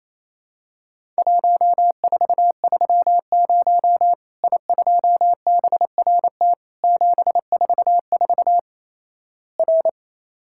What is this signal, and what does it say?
Morse code, heard as a keyed tone, about 27 wpm, 700 Hz. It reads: 1430 I2BRT 744 R
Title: Worked All Europe DX Contest - QTC Exchange - 28wpm